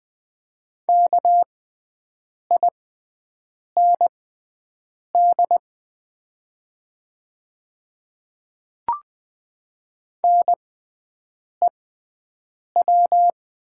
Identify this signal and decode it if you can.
Morse code, heard as a keyed tone, about 20 wpm, spaced as for 7 wpm, 700 Hz. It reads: KIND NEW
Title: 6x Character Spacing - Top 200 Words - 20wpm